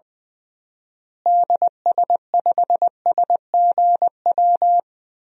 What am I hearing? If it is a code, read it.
Morse code, heard as a keyed tone, about 20 wpm, 700 Hz. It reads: DS5SGW
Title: DXCC-100 Call Signs - 6 Characters - Repeated 3x - 20wpm